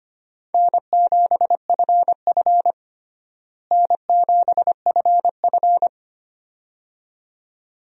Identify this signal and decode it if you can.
Morse code, heard as a keyed tone, about 25 wpm, 700 Hz. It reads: N7FF N7FF
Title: USA Call Signs - 4 Characters - Repeated 2x - 25wpm